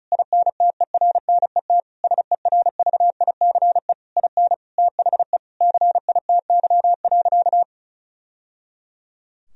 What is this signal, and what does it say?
Morse code, heard as a keyed tone, about 36 wpm, 700 Hz. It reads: INTERNET SERVICE IN THE CITY.
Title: Sentences from Top 1000 Words - Review Part 4 - 35wpm